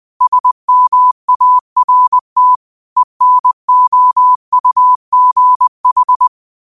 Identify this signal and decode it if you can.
Morse code, heard as a keyed tone, about 20 wpm, 1000 Hz. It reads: SMART ENOUGH
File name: qotdfiles / 20wpm